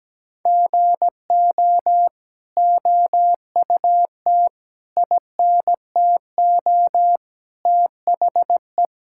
Morse code, heard as a keyed tone, about 17 wpm, 700 Hz. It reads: GO OUT INTO THE
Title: Sentences from Top 1000 Words - Review Part 5 - 17wpm